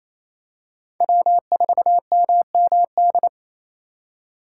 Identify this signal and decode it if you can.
Morse code, heard as a keyed tone, about 29 wpm, 700 Hz. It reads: W4MMD
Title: K1USN SST QSO Practice - 28wpm